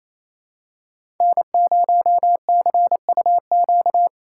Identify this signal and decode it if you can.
Morse code, heard as a keyed tone, about 27 wpm, 700 Hz. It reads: N0CUQ
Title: USA Callsign DE Callsign - 5 Characters - 28wpm